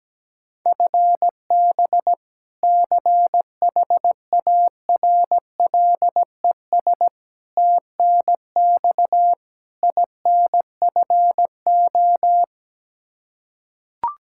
Morse code, heard as a keyed tone, about 17 wpm, 700 Hz. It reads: FB CHARLES TNX INFO
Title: Practice QSO - Exchange 3 - More Info - 17wpm